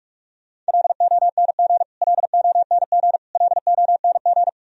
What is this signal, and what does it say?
Morse code, heard as a keyed tone, about 44 wpm, 700 Hz. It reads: LONG LONG LONG
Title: Top 100 Words Repeated 45wpm